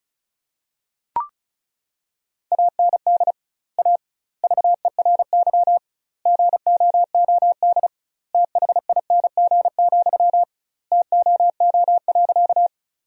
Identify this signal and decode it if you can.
Morse code, heard as a keyed tone, about 36 wpm, 700 Hz. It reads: AND A VERY GOOD THING, TOO.
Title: 2x Word-Spacing - Sentences from Top 200 Words - 35wpm